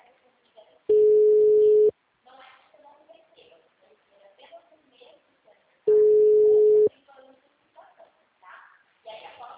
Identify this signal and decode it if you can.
Ringback tone (Europe, 425 Hz), with speech.
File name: out- -2026- - - .396